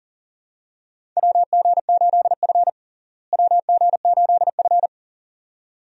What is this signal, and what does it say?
Morse code, heard as a keyed tone, about 37 wpm, 700 Hz. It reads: WG8F WG8F
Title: USA Call Signs - 4 Characters - Repeated 2x - 40wpm